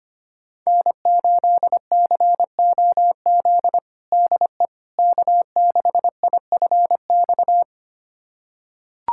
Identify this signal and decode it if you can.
Morse code, heard as a keyed tone, about 25 wpm, 700 Hz. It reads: N8COZ DE K6IFX
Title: USA Callsign DE Callsign - 5 Characters - 25wpm